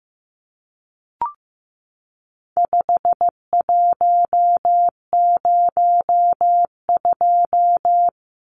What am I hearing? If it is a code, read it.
Morse code, heard as a keyed tone, about 15 wpm, 700 Hz. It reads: 5102
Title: Numbers - 4-Digit 15wpm